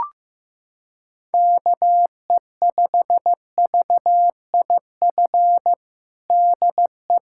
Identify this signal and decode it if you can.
Morse code, heard as a keyed tone, about 16 wpm, 700 Hz. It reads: KE5VIF DE
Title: USA Callsign DE Callsign - 6 Characters - 15wpm